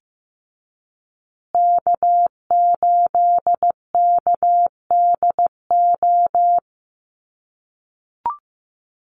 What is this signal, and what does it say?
Morse code, heard as a keyed tone, about 15 wpm, 700 Hz. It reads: K8KDO
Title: USA Call Signs Repeated 2x 15wpm